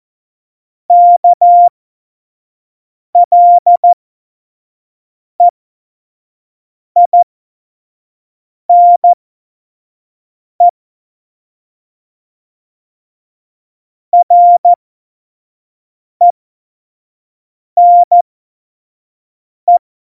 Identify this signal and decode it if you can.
Morse code, heard as a keyed tone, about 14 wpm, spaced as for 5 wpm, 700 Hz. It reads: KLEINE RENE
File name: Morsekurs / Witze / Witz 23. Januar 2018, 25 F70